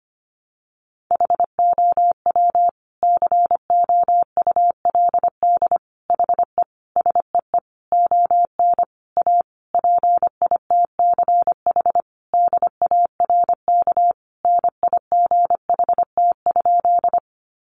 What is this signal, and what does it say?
Morse code, heard as a keyed tone, about 25 wpm, 700 Hz. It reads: HOW COULD HE SEE ON A PITCH DARK NIGHT?
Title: Sentences from Top 1300 Words 25wpm